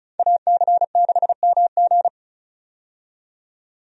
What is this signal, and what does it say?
Morse code, heard as a keyed tone, about 34 wpm, 700 Hz. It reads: AC6MG
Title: USA Call Signs - 5 Characters - 35wpm